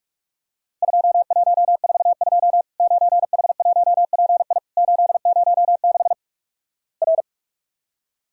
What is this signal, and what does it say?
Morse code, heard as a keyed tone, about 44 wpm, 700 Hz. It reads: 2142 9H1PI 806 R
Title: Worked All Europe DX Contest - QTC Exchange - 45wpm